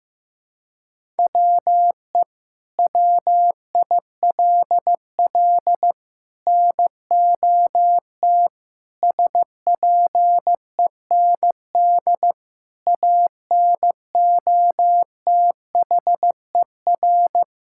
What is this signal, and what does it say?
Morse code, heard as a keyed tone, about 15 wpm, 700 Hz. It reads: WE WILL NOT SPEND ANOTHER
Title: Sentences from Top 1000 Words - Review Part 5 - 15wpm